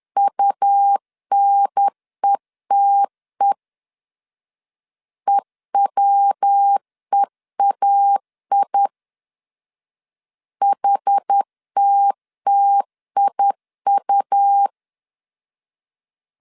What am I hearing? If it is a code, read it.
Morse code, heard as a keyed tone, about 10 wpm, 800 Hz. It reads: UNETE EWEAI HTTIU
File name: CW / DET / 1 3 1500C S B 10wpm 3335